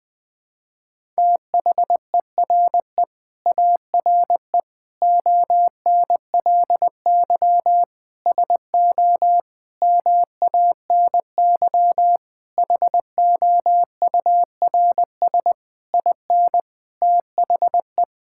Morse code, heard as a keyed tone, about 20 wpm, 700 Hz. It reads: THERE ARE ONLY SO MANY HOURS IN THE
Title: Sets of 9 Words - 20wpm